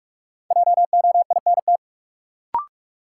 Morse code, heard as a keyed tone, about 47 wpm, 700 Hz. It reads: JOINT
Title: Sentences from Top 1300 Words - New Word Drill - 45wpm